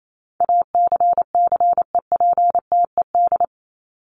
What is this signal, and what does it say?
Morse code, heard as a keyed tone, about 28 wpm, 700 Hz. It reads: ACCEPTED
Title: ED Suffix - Top 100 Words with Base Words 28wpm